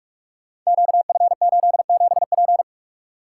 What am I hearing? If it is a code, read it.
Morse code, heard as a keyed tone, about 44 wpm, 700 Hz. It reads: QF87P
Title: Code Groups - 5 Letter:Numbers - 45wpm